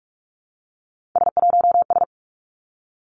Morse code, heard as a keyed tone, about 46 wpm, 700 Hz. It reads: S1S
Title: Three Challenging Letter-Number Combinations 45wpm